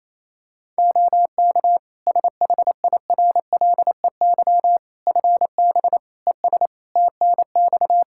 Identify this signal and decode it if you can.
Morse code, heard as a keyed tone, about 27 wpm, 700 Hz. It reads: OK SHIRLEY FB ES TNX
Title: Practice QSO - Exchange 2 - Info - 28wpm